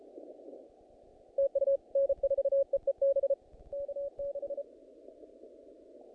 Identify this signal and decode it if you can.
Morse code, heard as a keyed tone, about 33 wpm, 560 Hz. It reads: TU N4EEB K6
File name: contests / naqp-cw-k5tr-op-k5pi- -2000